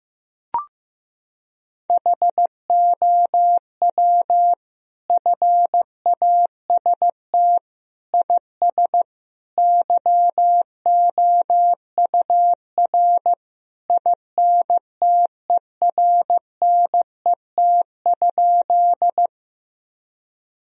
Morse code, heard as a keyed tone, about 15 wpm, 700 Hz. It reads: HOW FAST IS YOUR INTERNET?
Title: Sentences from Top 1000 Words - Review Part 4 - 15wpm